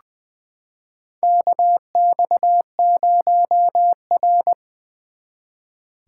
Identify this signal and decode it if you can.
Morse code, heard as a keyed tone, about 20 wpm, 700 Hz. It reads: KX0R
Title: 2023 SOTA Activator Call Signs - 20wpm